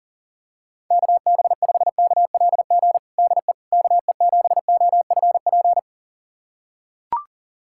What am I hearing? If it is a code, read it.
Morse code, heard as a keyed tone, about 37 wpm, 700 Hz. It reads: KB5KLG DE KE7OFP
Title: USA Callsign DE Callsign 40wpm